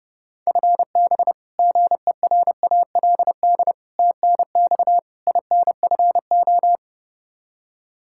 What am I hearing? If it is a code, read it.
Morse code, heard as a keyed tone, about 30 wpm, 700 Hz. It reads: FB GERALD TNX INFO
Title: Practice QSO - Exchange 3 - More Info - 30wpm